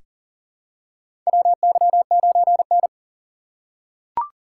Morse code, heard as a keyed tone, about 40 wpm, 700 Hz. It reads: WY9N
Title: USA Call Signs - 4 Characters - 4x Character Spacing - 40wpm